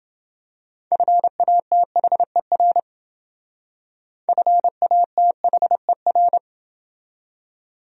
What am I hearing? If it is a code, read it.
Morse code, heard as a keyed tone, about 28 wpm, 700 Hz. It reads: FATHER FATHER